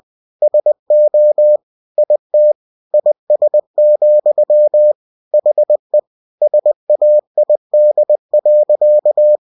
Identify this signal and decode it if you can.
Morse code, heard as a keyed tone, about 20 wpm, 600 Hz. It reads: SO IT IS, HE SAID.
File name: cw-books / winnie-the-pooh / winnie-20wpm-page-0049